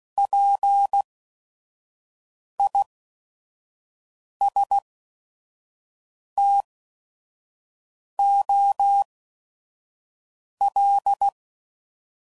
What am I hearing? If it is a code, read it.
Morse code, heard as a keyed tone, about 16 wpm, spaced as for 5 wpm, 800 Hz. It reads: PISTOL
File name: Prefixes / By First Letter / Ps